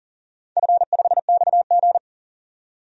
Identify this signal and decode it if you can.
Morse code, heard as a keyed tone, about 37 wpm, 700 Hz. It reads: F5XG
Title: DXCC-100 Call Signs - 4 Characters - 40wpm